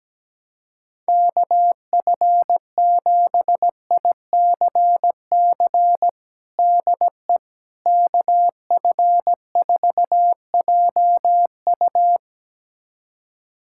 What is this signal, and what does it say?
Morse code, heard as a keyed tone, about 17 wpm, 700 Hz. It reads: KF7ICC DE KF4JU
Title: USA Callsign DE Callsign 17wpm